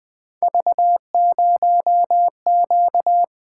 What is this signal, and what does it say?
Morse code, heard as a keyed tone, about 20 wpm, 700 Hz. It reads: V0Q
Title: Three Letter-Number Combinations - 20wpm